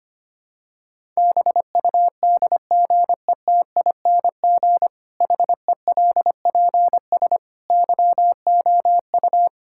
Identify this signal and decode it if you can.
Morse code, heard as a keyed tone, about 25 wpm, 700 Hz. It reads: BUDGETING HELPS YOU
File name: practice / 25wpm / Difficult Sentences with Hints - 25wpm